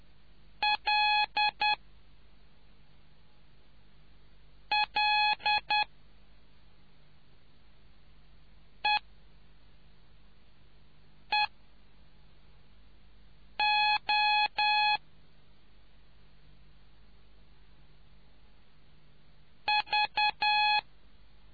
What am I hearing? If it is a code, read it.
Morse code, heard as a keyed tone, about 10 wpm, 840 Hz. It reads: L L E E O V